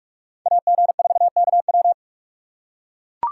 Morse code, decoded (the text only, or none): AG4KW